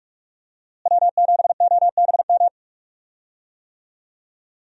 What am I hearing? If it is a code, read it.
Morse code, heard as a keyed tone, about 45 wpm, 700 Hz. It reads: W7OBM
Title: USA Call Signs - 5 Characters - Repeated 2x - 45wpm